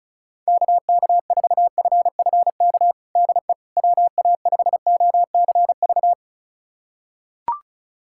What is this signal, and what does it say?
Morse code, heard as a keyed tone, about 36 wpm, 700 Hz. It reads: KK4FFK DE WA5OCV
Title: USA Callsign DE Callsign - 6 Characters - 35wpm